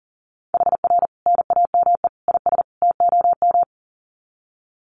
Morse code, heard as a keyed tone, about 40 wpm, 700 Hz. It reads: HR NAME IS TOM
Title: Common QSO Phrases 40wpm